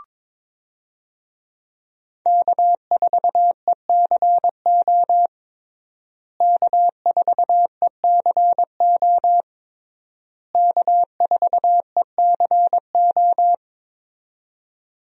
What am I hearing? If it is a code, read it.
Morse code, heard as a keyed tone, about 22 wpm, 700 Hz. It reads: K4ECO K4ECO K4ECO
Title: USA Call Signs - 5 Characters - Repeated 3x - 22wpm